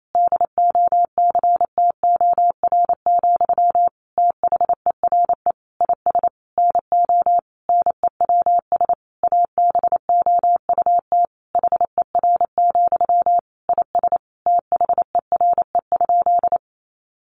Morse code, heard as a keyed tone, about 28 wpm, 700 Hz. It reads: DOCTOR, THERE IS NO NEWS ABOUT HER, IS THERE?
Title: Sentences from Top 1500 Words 28wpm